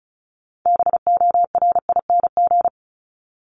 Morse code, heard as a keyed tone, about 34 wpm, 700 Hz. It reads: BORING
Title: Top 1501-1750 Words 35wpm